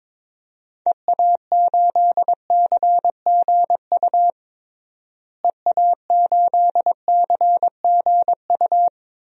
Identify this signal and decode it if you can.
Morse code, heard as a keyed tone, about 22 wpm, 700 Hz. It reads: EA8CGU EA8CGU
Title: DXCC-100 Call Signs - 6 Characters - Repeated 2x - 22wpm